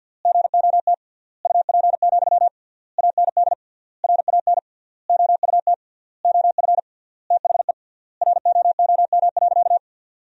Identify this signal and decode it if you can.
Morse code, heard as a keyed tone, about 49 wpm, 700 Hz. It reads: GOT UP, AND RAN OUT OF THE ROOM.
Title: 3x Word-Spacing - Sentences from Top 500 Words - 50wpm